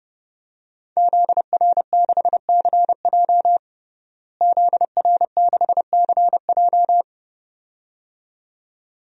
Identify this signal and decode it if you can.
Morse code, heard as a keyed tone, about 32 wpm, 700 Hz. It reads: ZR6CJ ZR6CJ
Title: DXCC-100 Call Signs - 5 Characters - Repeated 2x - 30wpm